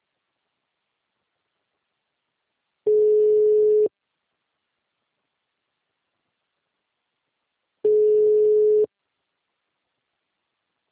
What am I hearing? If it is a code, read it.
Ringback tone (Europe, 425 Hz).
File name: out- -2012- - - .4880